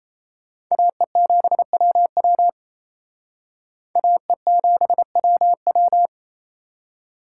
Morse code, heard as a keyed tone, about 35 wpm, 700 Hz. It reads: AE7WW AE7WW